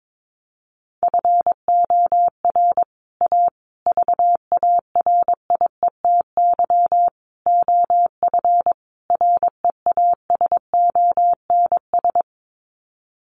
Morse code, heard as a keyed tone, about 22 wpm, 700 Hz. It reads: FOR A VARIETY OF REASONS
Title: Sets of 5 Words Encore 22wpm